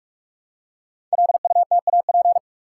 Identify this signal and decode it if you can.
Morse code, heard as a keyed tone, about 44 wpm, 700 Hz. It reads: LUTAP
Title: Code Groups - 5 Letters - 45wpm